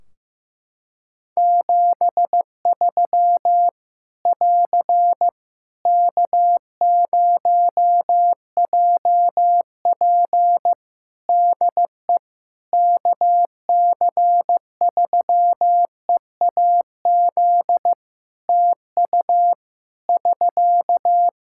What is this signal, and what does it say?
Morse code, heard as a keyed tone, about 15 wpm, 700 Hz. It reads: 73 <AR> K0JP DE KC3EAZ TU <SK>
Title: Practice QSO - Exchange 4 - Ending - 15wpm